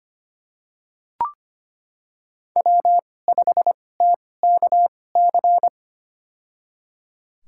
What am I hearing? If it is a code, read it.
Morse code, heard as a keyed tone, about 25 wpm, spaced as for 18 wpm, 700 Hz. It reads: W5TKC
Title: USA Call Signs - 5 Characters - 2x Character Spacing - 25wpm